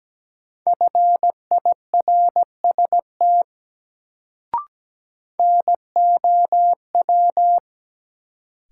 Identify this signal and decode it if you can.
Morse code, heard as a keyed tone, about 17 wpm, 700 Hz. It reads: FIRST NOW
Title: Sets of 2 Words - New Word Drill - 17wpm